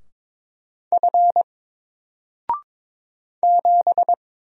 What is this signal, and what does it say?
Morse code, heard as a keyed tone, about 22 wpm, 700 Hz. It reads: F 7
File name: Single-Letter-Number-22wpm